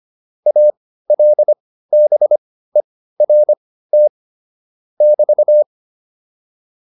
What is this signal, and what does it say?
Morse code, heard as a keyed tone, about 25 wpm, spaced as for 15 wpm, 600 Hz. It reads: ALBERT <BT>